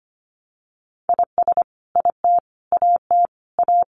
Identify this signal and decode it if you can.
Morse code, heard as a keyed tone, about 25 wpm, 700 Hz. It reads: IS IT AT A
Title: Sentences from Top 1200 Words 25wpm